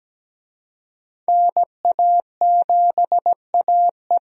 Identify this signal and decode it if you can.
Morse code, heard as a keyed tone, about 17 wpm, 700 Hz. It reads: NA7AE
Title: USA Call Signs - 5 Characters - Repeated 2x - 17wpm